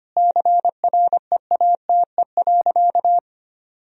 Morse code, heard as a keyed tone, about 25 wpm, 700 Hz. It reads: CREATE.
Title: Sentences from Top 1000 Words - Review Part 3 - 25wpm